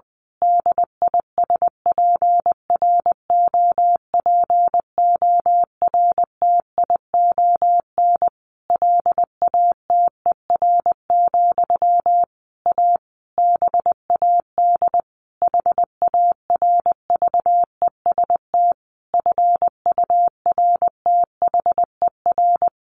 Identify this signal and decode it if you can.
Morse code, heard as a keyed tone, about 20 wpm, 700 Hz. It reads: DISPROPORTION LATER, A BAD HARVEST FURTHER
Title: A Sound Of Thunder 20wpm